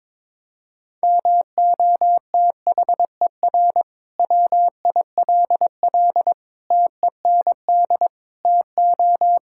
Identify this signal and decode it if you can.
Morse code, heard as a keyed tone, about 21 wpm, 700 Hz. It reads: MOTHER WILL TEND TO
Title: Sentences from Top 1000 Words - Review Part 5 - 22wpm